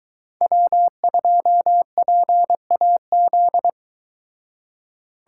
Morse code, heard as a keyed tone, about 22 wpm, 700 Hz. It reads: W2PAZ